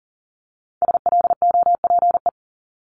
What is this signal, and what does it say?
Morse code, heard as a keyed tone, about 38 wpm, 700 Hz. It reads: SLOPE
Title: Multi-letter Phonemes 40wpm